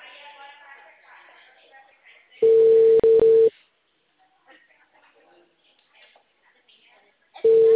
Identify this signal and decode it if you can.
Ringback tone (China, 450 Hz), with speech.
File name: out- -2004- - - .9743